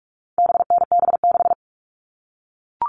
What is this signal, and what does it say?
Morse code, heard as a keyed tone, about 44 wpm, 700 Hz. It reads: BNB6
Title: All Cut Numbers - 4-Digit 45wpm